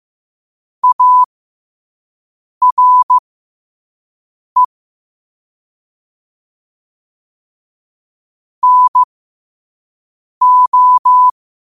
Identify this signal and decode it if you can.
Morse code, heard as a keyed tone, about 15 wpm, spaced as for 5 wpm, 1000 Hz. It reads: ARE NO
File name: qotdfiles / 5wpm